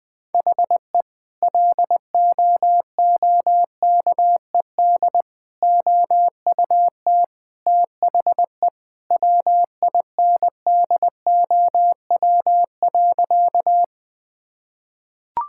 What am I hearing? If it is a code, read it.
Morse code, heard as a keyed tone, about 21 wpm, 700 Hz. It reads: HE LOOKED OUT THE WINDOW.
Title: Sentences with 2-Word Phrases - Spoken Beginning - 20wpm